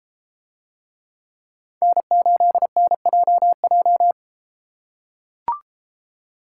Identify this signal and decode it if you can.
Morse code, heard as a keyed tone, about 33 wpm, 700 Hz. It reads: N8NJJ